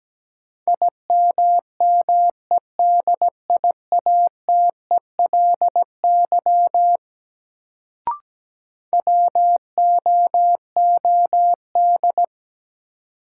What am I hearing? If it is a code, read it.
Morse code, heard as a keyed tone, about 17 wpm, 700 Hz. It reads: IMMEDIATELY WOOD
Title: Sentences from Top 1100 Words - New Word Drill Alt - 17wpm